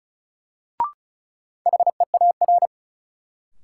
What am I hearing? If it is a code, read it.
Morse code, heard as a keyed tone, about 35 wpm, 700 Hz. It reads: HEAR
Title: Sentences from Top 300 Words - New Word Drill - 35wpm